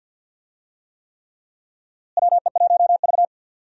Morse code, heard as a keyed tone, about 51 wpm, 700 Hz. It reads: WE1V
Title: USA Call Signs - 4 Characters - Repeated 3x - 50wpm